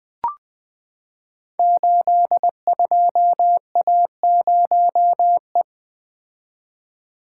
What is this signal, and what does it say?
Morse code, heard as a keyed tone, about 20 wpm, 700 Hz. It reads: 82A0E
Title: Code Groups - 5 Letter:Numbers - 20wpm